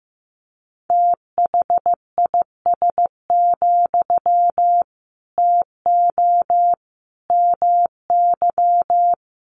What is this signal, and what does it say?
Morse code, heard as a keyed tone, about 15 wpm, 700 Hz. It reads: THIS, TO MY
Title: Sentences from Top 1500 Words 15wpm